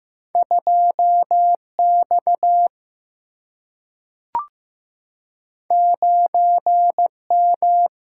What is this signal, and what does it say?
Morse code, heard as a keyed tone, about 15 wpm, 700 Hz. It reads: 2 X 9 M
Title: Single-Letter Single-Number 15wpm